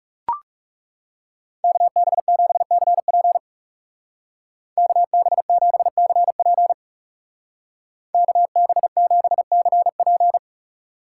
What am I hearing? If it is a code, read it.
Morse code, heard as a keyed tone, about 35 wpm, 700 Hz. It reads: KB7CP KB7CP KB7CP
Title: Speed Racing - USA 2x2 Call Signs 30wpm